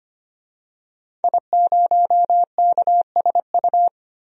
Morse code, heard as a keyed tone, about 25 wpm, 700 Hz. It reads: I0KSU
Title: DXCC-100 Call Signs - 5 Characters - 25wpm